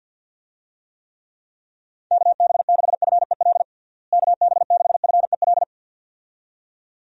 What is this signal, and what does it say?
Morse code, heard as a keyed tone, about 50 wpm, 700 Hz. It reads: KB6FEL KB6FEL
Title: USA Call Signs - 6 Characters - Repeated 2x - 50wpm